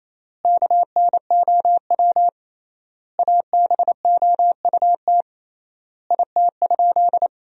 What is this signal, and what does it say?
Morse code, heard as a keyed tone, about 28 wpm, 700 Hz. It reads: KNOW ABOUT IT?
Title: 3x Word-Spacing - Sentences from Top 100 Words - 28wpm